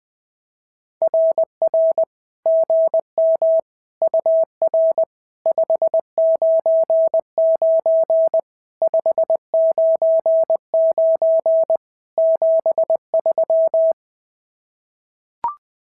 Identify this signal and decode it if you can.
Morse code, heard as a keyed tone, about 20 wpm, 650 Hz. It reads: RR GM UR 599 599 73
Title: SOTA QSO Practice 20wpm